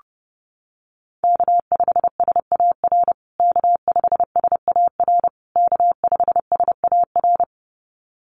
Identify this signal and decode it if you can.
Morse code, heard as a keyed tone, about 30 wpm, 700 Hz. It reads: K5SAR K5SAR K5SAR
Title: USA Call Signs Repeated 3x 30wpm